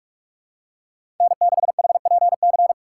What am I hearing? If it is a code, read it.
Morse code, heard as a keyed tone, about 47 wpm, 700 Hz. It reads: N6HPC
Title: USA Call Signs - 5 Characters - 6x Character Spacing - 45wpm